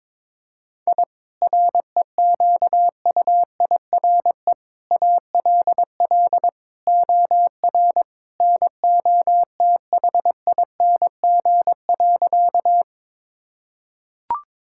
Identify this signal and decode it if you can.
Morse code, heard as a keyed tone, about 22 wpm, 700 Hz. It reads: I REQUIRE ALL OR NOTHING.
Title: Sentences from Top 1000 Words - Review Part 6 - 22wpm